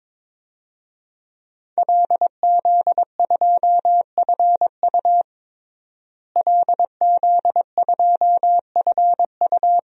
Morse code, heard as a keyed tone, about 22 wpm, 700 Hz. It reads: LZ2FU LZ2FU
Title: DXCC-100 Call Signs - 5 Characters - Repeated 2x - 22wpm